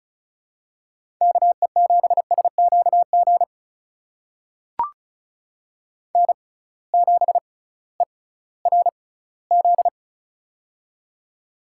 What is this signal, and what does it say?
Morse code, heard as a keyed tone, about 36 wpm, spaced as for 19 wpm, 700 Hz. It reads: KE7SQG N7ERZ
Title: USA Call Signs - 6x Character Spacing - 35wpm